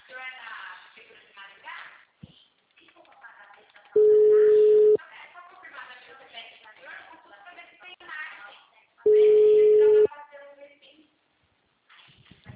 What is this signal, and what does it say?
Ringback tone (Europe, 425 Hz), with speech.
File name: out- -2001- - - .90361